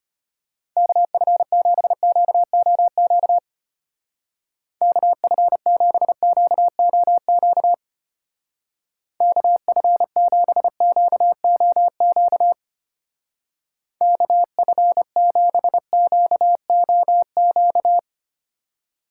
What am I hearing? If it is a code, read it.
Morse code, heard as a keyed tone, about 31 wpm, 700 Hz. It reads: KF7QOQ KF7QOQ KF7QOQ KF7QOQ
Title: Speed Racing - USA 2x3 Call Signs 25wpm